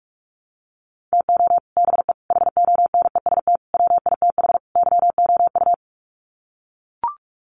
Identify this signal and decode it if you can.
Morse code, heard as a keyed tone, about 44 wpm, 700 Hz. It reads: TO BE HONEST WITH YOU
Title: Sets of 5 Words Encore 45wpm